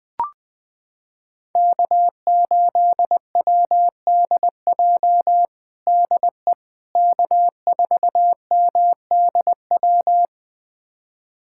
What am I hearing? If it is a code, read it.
Morse code, heard as a keyed tone, about 21 wpm, 700 Hz. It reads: K8WDJ DE K4MDW
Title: USA Callsign DE Callsign - 5 Characters - 20wpm